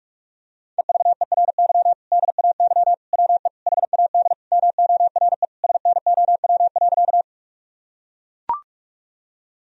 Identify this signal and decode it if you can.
Morse code, heard as a keyed tone, about 41 wpm, 700 Hz. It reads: EVERY DAY WE HAD MORE SNOW.
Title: Sentences from Top 1000 Words - Review Part 3 - 45wpm